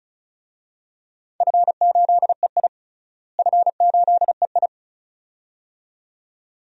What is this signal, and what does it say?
Morse code, heard as a keyed tone, about 36 wpm, 700 Hz. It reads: F8EI F8EI
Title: DXCC-100 Call Signs - 4 Characters - Repeated 2x - 35wpm